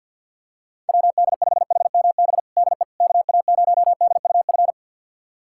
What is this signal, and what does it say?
Morse code, heard as a keyed tone, about 47 wpm, 700 Hz. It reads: WD5HMB DE KA0DUF
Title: USA Callsign DE Callsign 50wpm